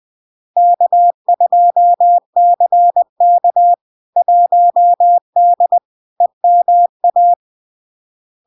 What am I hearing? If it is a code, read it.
Morse code, heard as a keyed tone, about 20 wpm, 700 Hz. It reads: K2CK 1D EMA
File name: LARK-FD-2022-Log-in-CW-20wpm-700-Hz